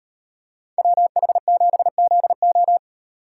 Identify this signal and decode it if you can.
Morse code, heard as a keyed tone, about 38 wpm, 700 Hz. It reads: WH7ZO